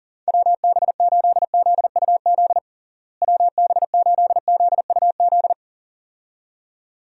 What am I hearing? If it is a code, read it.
Morse code, heard as a keyed tone, about 37 wpm, 700 Hz. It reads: WB8ZUZ WB8ZUZ
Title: USA Call Signs - 6 Characters - Repeated 2x - 40wpm